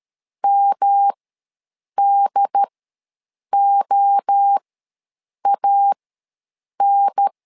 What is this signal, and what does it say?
Morse code, heard as a keyed tone, about 13 wpm, spaced as for 7 wpm, 800 Hz. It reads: MDOAN
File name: LEZIONI / 3 MWRO / 3 MWRO ws 12wpm 1914